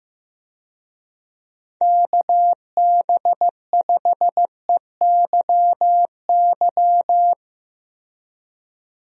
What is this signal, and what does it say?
Morse code, heard as a keyed tone, about 15 wpm, 700 Hz. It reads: KB5EYY
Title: USA Call Signs - 6 Characters - Repeated 2x - 15wpm